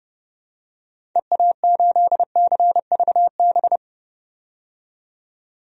Morse code, heard as a keyed tone, about 30 wpm, 700 Hz. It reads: EA8CVB
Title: DXCC-100 Call Signs - 6 Characters - 30wpm